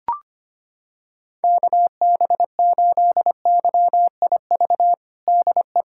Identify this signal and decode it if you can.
Morse code, heard as a keyed tone, about 25 wpm, 700 Hz. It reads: KB8YIV DE
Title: USA Callsign DE Callsign - 6 Characters - 25wpm